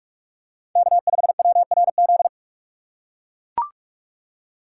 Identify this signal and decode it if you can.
Morse code, heard as a keyed tone, about 41 wpm, 700 Hz. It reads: K5WRZ
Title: USA Call Signs - 5 Characters - 2x Character Spacing - 45wpm